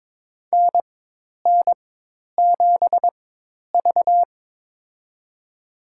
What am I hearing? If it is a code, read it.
Morse code, heard as a keyed tone, about 22 wpm, spaced as for 10 wpm, 700 Hz. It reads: NN7V
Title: USA Call Signs - 4 Characters - 4x Character Spacing - 22wpm